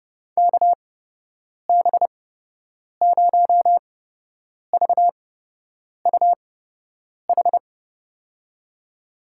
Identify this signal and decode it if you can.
Morse code, heard as a keyed tone, about 30 wpm, spaced as for 8 wpm, 700 Hz. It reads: KB0VUH
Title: USA Call Signs - 8x Character Spacing - 30wpm